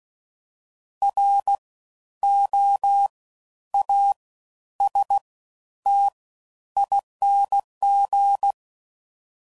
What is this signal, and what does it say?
Morse code, heard as a keyed tone, about 16 wpm, spaced as for 10 wpm, 800 Hz. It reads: ROASTING
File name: Suffixes / Special INGs / Fast Ings